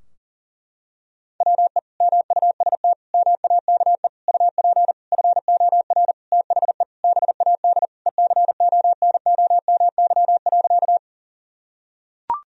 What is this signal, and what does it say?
Morse code, heard as a keyed tone, about 37 wpm, 700 Hz. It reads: WE MUST MAKE UP FOR THE BAD ECONOMY.
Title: Sentences from Top 1000 Words - Review Part 3 - 40wpm